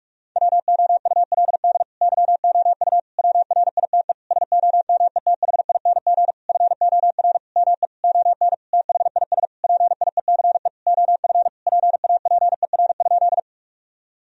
Morse code, heard as a keyed tone, about 46 wpm, 700 Hz. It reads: WOULD YOU WRITE SOMETHING FOR ME ON THIS PIECE OF PAPER?
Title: Sentences from Top 1000 Words - Review Part 2 - 45wpm